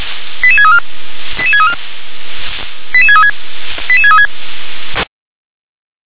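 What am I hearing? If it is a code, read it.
Five-tone selective call (ZVEI), heard as quick runs of five tones, tones of about 70 ms. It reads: 88634 88634 88636 88636
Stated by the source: ZVEI five-tone call received by SDR.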